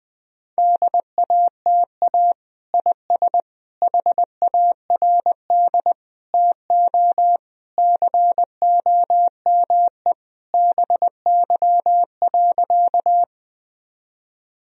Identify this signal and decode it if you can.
Morse code, heard as a keyed tone, about 21 wpm, 700 Hz. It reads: DATA IS HARD TO COME BY.
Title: Sentences from Top 1000 Words - Review Part 3 - 20wpm